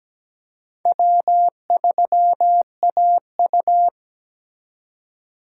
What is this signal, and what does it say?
Morse code, heard as a keyed tone, about 17 wpm, 700 Hz. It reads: W3AU
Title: USA Call Signs - 2x Character Spacing - 17wpm